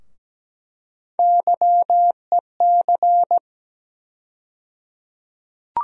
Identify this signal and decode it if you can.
Morse code, heard as a keyed tone, about 17 wpm, 700 Hz. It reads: YEC
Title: USA Call Sign Postfixes - 3 Letters 17wpm